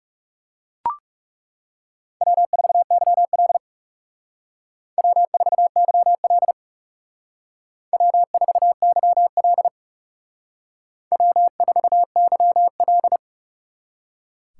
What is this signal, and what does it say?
Morse code, heard as a keyed tone, about 36 wpm, 700 Hz. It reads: W4YL W4YL W4YL W4YL
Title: Speed Racing - USA 1x2 Call Signs 30wpm